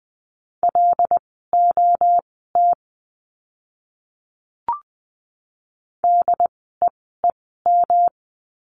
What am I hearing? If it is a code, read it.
Morse code, heard as a keyed tone, about 20 wpm, spaced as for 15 wpm, 700 Hz. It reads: LOT DEEM
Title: 108 Lesson 18f - Letter m - Words 20wpm